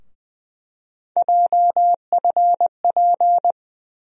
Morse code, heard as a keyed tone, about 21 wpm, 700 Hz. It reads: JFP
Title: USA Call Sign Postfixes - 3 Letters 20wpm